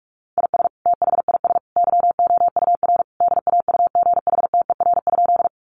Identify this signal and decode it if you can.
Morse code, heard as a keyed tone, about 44 wpm, 700 Hz. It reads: IS THIS YOUR DAUGHTER?
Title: Sentences from Top 1500 Words 45wpm